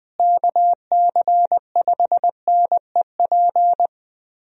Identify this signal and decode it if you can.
Morse code, heard as a keyed tone, about 20 wpm, 700 Hz. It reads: KC5NEP
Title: USA Call Signs - 6 Characters - Repeated 2x - 20wpm